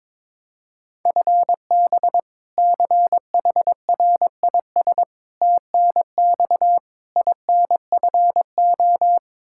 Morse code, heard as a keyed tone, about 21 wpm, 700 Hz. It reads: FB CHRIS TNX INFO
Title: Practice QSO - Exchange 3 - More Info - 22wpm